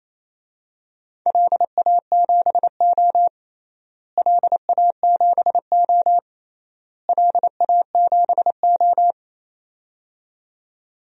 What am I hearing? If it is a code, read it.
Morse code, heard as a keyed tone, about 28 wpm, 700 Hz. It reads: LA7O LA7O LA7O
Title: DXCC-100 Call Signs - 4 Characters - Repeated 3x - 28wpm